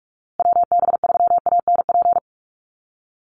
Morse code, heard as a keyed tone, about 44 wpm, 700 Hz. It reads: WB3ANP
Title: USA Call Signs Repeated 2x 45wpm